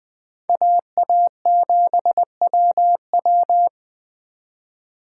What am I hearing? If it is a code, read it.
Morse code, heard as a keyed tone, about 21 wpm, 700 Hz. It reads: AA7WW
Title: Speed Racing - USA 2x2 Call Signs 17wpm